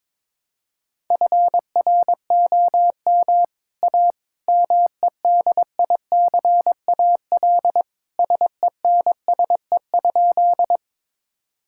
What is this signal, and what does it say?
Morse code, heard as a keyed tone, about 21 wpm, 700 Hz. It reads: FROM A MEDICAL SENSE?
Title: Sentences from Top 1000 Words - Review Part 5 - 22wpm